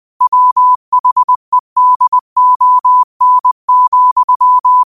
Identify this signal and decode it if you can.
Morse code, heard as a keyed tone, about 20 wpm, 1000 Hz. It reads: WHEDON,
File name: 20wpm